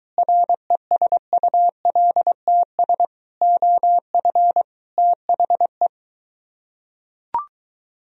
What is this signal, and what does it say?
Morse code, heard as a keyed tone, about 22 wpm, 700 Hz. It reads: RESULTS OF THE